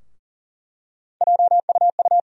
Morse code, heard as a keyed tone, about 40 wpm, 700 Hz. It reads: JUU
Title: USA Call Sign Postfixes - 3 Letters 40wpm